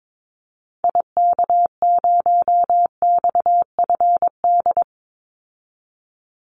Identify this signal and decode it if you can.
Morse code, heard as a keyed tone, about 22 wpm, 700 Hz. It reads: IK0XFD
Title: Call Signs Repeated 3x - 22wpm